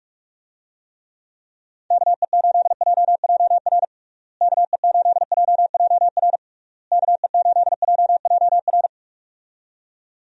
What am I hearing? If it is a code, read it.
Morse code, heard as a keyed tone, about 46 wpm, 700 Hz. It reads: KE8JJR KE8JJR KE8JJR
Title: USA Call Signs - 6 Characters - Repeated 3x - 45wpm